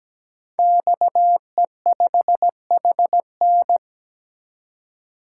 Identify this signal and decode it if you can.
Morse code, heard as a keyed tone, about 17 wpm, 700 Hz. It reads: XE5HN
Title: Code Groups - 5 Letter:Numbers - 17wpm